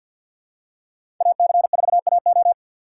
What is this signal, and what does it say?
Morse code, heard as a keyed tone, about 52 wpm, 700 Hz. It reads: AC4AO
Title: USA Call Signs - 5 Characters - 6x Character Spacing - 50wpm